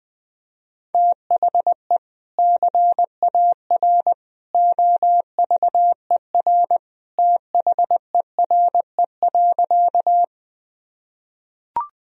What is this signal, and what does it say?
Morse code, heard as a keyed tone, about 19 wpm, 700 Hz. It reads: THE CAR OVER THERE.
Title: Sentences from Top 1000 Words - Review Part 1 - 20wpm